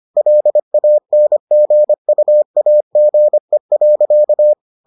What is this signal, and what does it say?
Morse code, heard as a keyed tone, about 25 wpm, 600 Hz. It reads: LANGUAGE.